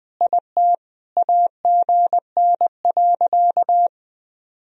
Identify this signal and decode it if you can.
Morse code, heard as a keyed tone, about 20 wpm, 700 Hz. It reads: IT AGN.
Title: Sentences from Top 300 Words and Abbreviations 20wpm